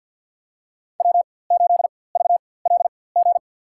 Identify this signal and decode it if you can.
Morse code, heard as a keyed tone, about 51 wpm, spaced as for 23 wpm, 700 Hz. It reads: W8VLG
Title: USA Call Signs - 5 Characters - 4x Character Spacing - 50wpm